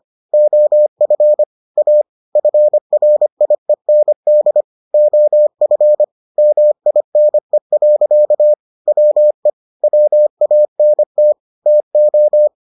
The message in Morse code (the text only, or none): OF A FRIEND OF MINE. WE WANT TO